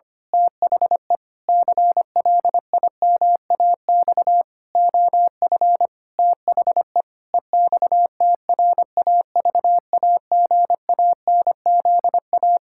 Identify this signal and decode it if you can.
Morse code, heard as a keyed tone, about 25 wpm, 700 Hz. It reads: THE CLIMAX OF THE EXTRAVAGANZA
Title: Pangram Challenge v2 - 25wpm